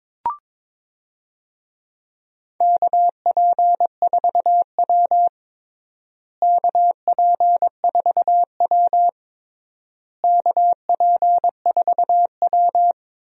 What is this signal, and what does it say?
Morse code, heard as a keyed tone, about 22 wpm, 700 Hz. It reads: KP4W KP4W KP4W
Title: DXCC-100 Call Signs - 4 Characters - Repeated 3x - 22wpm